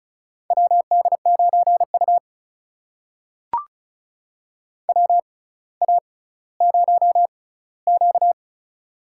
Morse code, heard as a keyed tone, about 35 wpm, spaced as for 18 wpm, 700 Hz. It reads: WD9U WA0Q
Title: USA Call Signs - 4 Characters - 6x Character Spacing - 35wpm